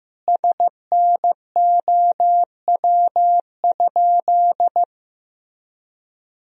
Morse code, heard as a keyed tone, about 15 wpm, 700 Hz. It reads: SNOW?
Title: Sentences from Top 1000 Words - Review Part 3 - 15wpm